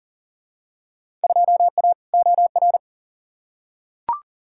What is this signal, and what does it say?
Morse code, heard as a keyed tone, about 37 wpm, 700 Hz. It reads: 2A OR
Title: Field Day Exchanges - 40wpm